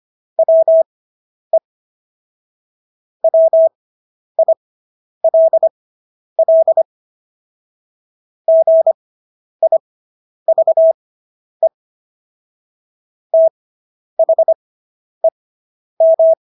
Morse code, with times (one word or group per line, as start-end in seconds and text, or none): 0.4-1.6 WE
3.2-6.8 WILL
8.5-11.7 GIVE
13.3-16.3 THEM